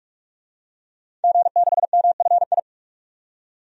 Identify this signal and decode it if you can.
Morse code, heard as a keyed tone, about 42 wpm, 700 Hz. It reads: G6MFI
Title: DXCC-100 Call Signs - 5 Characters - 45wpm